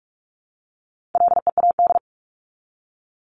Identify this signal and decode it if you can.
Morse code, heard as a keyed tone, about 44 wpm, 700 Hz. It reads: LEAD
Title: ING Suffix - Top 100 Words with Base Words 45wpm